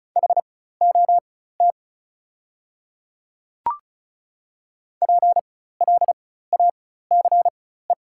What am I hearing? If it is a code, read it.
Morse code, heard as a keyed tone, about 36 wpm, spaced as for 16 wpm, 700 Hz. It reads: HOT PLACE
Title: 4x Character Spacing - Top 200 Words - 35wpm